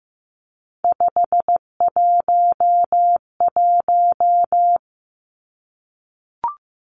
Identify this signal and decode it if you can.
Morse code, heard as a keyed tone, about 15 wpm, 700 Hz. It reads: 511
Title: Three Challenging Letter-Number Combinations 15wpm